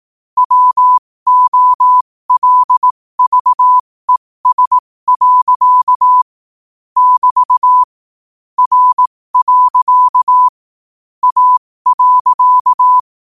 Morse code, heard as a keyed tone, about 18 wpm, spaced as for 15 wpm, 1000 Hz. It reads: WOLVES. <BT> R. A.
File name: qotdfiles / 15wpm